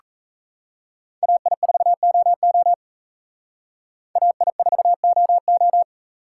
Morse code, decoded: AI4OO AI4OO

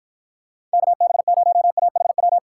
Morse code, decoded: KB0AHW

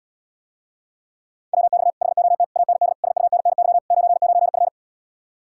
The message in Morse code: WX HOT AND SUNNY 99F

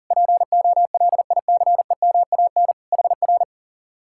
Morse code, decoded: POLICEMAN HR

